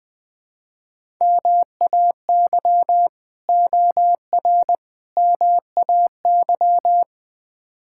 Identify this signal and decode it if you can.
Morse code, heard as a keyed tone, about 20 wpm, 700 Hz. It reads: MAY OR MAY
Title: Build A Phrase - 5-Word Phrases - 20wpm